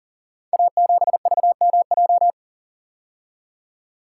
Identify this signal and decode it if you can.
Morse code, heard as a keyed tone, about 42 wpm, 690 Hz. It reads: A7VMJ
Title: Code Groups - 5 Letter:Numbers - 40wpm